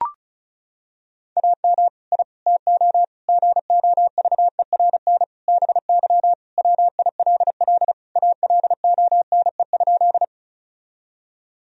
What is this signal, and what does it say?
Morse code, heard as a keyed tone, about 34 wpm, 700 Hz. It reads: AM I TO GOVERN BY WILL ALONE?
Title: Sentences from Top 1000 Words - Review Part 2 - 35wpm